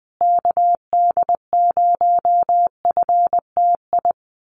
Morse code, heard as a keyed tone, about 20 wpm, 700 Hz. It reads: KD0FTI
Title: USA Call Signs Repeated 3x 20wpm